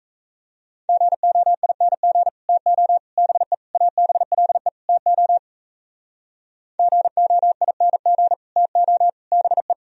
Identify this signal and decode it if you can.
Morse code, heard as a keyed tone, about 35 wpm, 700 Hz. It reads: GOING TO BE ABLE TO GOING TO BE